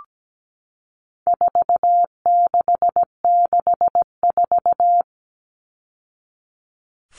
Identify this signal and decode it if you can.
Morse code, heard as a keyed tone, about 17 wpm, 700 Hz. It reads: 4664
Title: All Cut Numbers - 4-Digit 17wpm